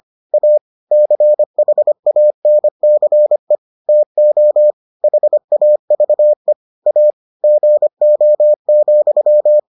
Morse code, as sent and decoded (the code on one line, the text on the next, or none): .- / -.-. .... .- -. -.-. / - --- / .... .- ...- . / .- / --. --- --..--
A CHANCE TO HAVE A GO,